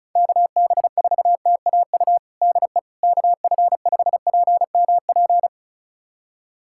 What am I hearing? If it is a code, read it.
Morse code, heard as a keyed tone, about 36 wpm, 700 Hz. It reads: KB4TAU DE KF5PMP
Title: USA Callsign DE Callsign - 6 Characters - 35wpm